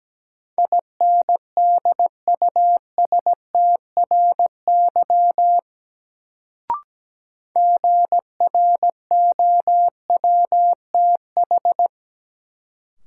Morse code, heard as a keyed tone, about 17 wpm, 700 Hz. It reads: INDUSTRY GROWTH
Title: Sentences from Top 1300 Words - New Word Drill Alt - 17wpm